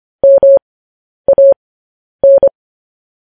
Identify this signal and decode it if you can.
Morse code, heard as a keyed tone, about 25 wpm, spaced as for 10 wpm, 550 Hz. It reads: MAN